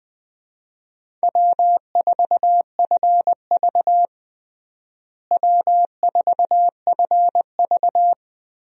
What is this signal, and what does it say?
Morse code, heard as a keyed tone, about 21 wpm, 700 Hz. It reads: W4FV W4FV
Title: USA Call Signs - 4 Characters - Repeated 2x - 20wpm